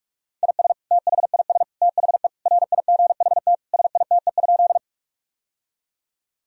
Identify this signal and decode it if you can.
Morse code, heard as a keyed tone, about 41 wpm, 700 Hz. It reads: IS THIS THE RIGHT SITE?
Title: Sentences from Top 1000 Words - Review Part 4 - 45wpm